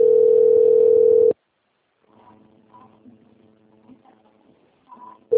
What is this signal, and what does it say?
Ringback tone (North America), 440 + 480 Hz.